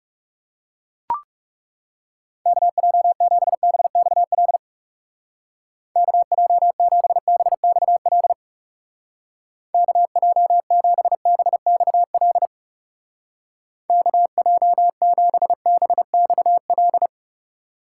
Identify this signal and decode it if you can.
Morse code, heard as a keyed tone, about 39 wpm, 700 Hz. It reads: KJ7BXL KJ7BXL KJ7BXL KJ7BXL